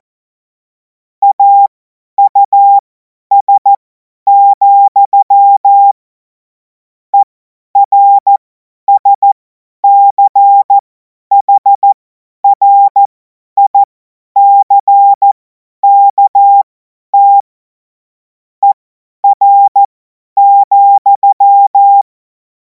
Morse code, heard as a keyed tone, about 14 wpm, spaced as for 10 wpm, 800 Hz. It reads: AUS, ERSCHRICKT ER,
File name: Morsekurs / Witze / Witz 14. April 2026 50 F70